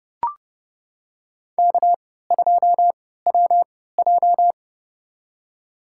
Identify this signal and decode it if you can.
Morse code, heard as a keyed tone, about 30 wpm, spaced as for 17 wpm, 700 Hz. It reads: K2WJ
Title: USA Call Signs - 4 Characters - 3x Character Spacing - 30wpm